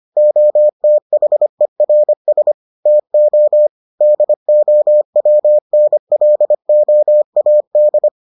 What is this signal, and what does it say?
Morse code, heard as a keyed tone, about 25 wpm, 600 Hz. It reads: OTHERS TO DOWNLOAD